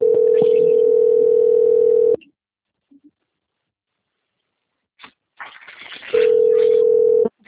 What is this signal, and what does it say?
Ringback tone (North America), 440 + 480 Hz.
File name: out- - - - - .92613